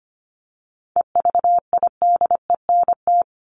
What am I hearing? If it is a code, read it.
Morse code, heard as a keyed tone, about 25 wpm, 700 Hz. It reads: EVIDENT
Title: LY Suffix - Top 100 Words with Base Words 25wpm